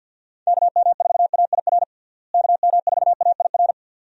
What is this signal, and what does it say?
Morse code, heard as a keyed tone, about 51 wpm, 700 Hz. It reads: KM4AIR KM4AIR
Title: USA Call Signs - 6 Characters - Repeated 2x - 50wpm